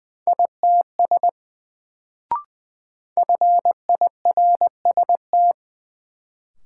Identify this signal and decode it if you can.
Morse code, heard as a keyed tone, about 20 wpm, 700 Hz. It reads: ITS FIRST
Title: Sets of 2 Words - New Word Drill - 20wpm